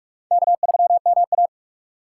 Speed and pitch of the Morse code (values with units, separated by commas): 44 wpm, 700 Hz